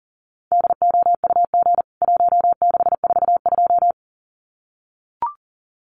Morse code, heard as a keyed tone, about 37 wpm, 700 Hz. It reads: DOUG 1642
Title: CWT Exchange 40wpm